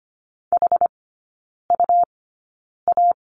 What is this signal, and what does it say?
Morse code, heard as a keyed tone, about 25 wpm, spaced as for 9 wpm, 700 Hz. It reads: HUA
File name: Lesson 17e - Letter u - Three Characters 25wpm